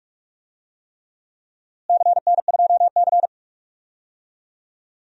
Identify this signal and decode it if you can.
Morse code, heard as a keyed tone, about 46 wpm, 700 Hz. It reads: CN2C
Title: DXCC-100 Call Signs - 4 Characters - Repeated 3x - 45wpm